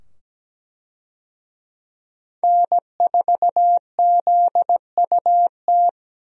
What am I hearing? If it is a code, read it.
Morse code, heard as a keyed tone, about 17 wpm, 700 Hz. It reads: N4ZUT
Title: USA Call Signs - 5 Characters - 17wpm